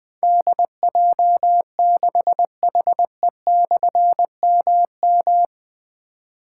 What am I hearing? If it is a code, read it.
Morse code, heard as a keyed tone, about 20 wpm, 700 Hz. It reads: DJ6HE/MM
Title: Difficult Call Signs 20wpm